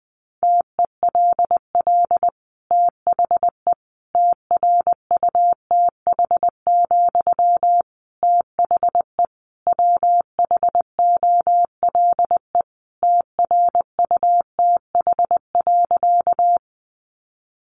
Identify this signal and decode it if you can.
Morse code, heard as a keyed tone, about 20 wpm, 700 Hz. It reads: TELL THE TRUTH, THE WHOLE TRUTH.
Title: Sentences from Top 800 Words 20wpm